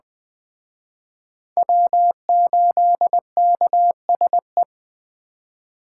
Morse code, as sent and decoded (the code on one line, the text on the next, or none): .-- ---.. -.- ... .
W8KSE